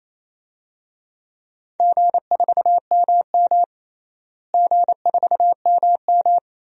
Morse code, heard as a keyed tone, about 28 wpm, 700 Hz. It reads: G4MM G4MM
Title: DXCC-100 Call Signs - 4 Characters - Repeated 2x - 28wpm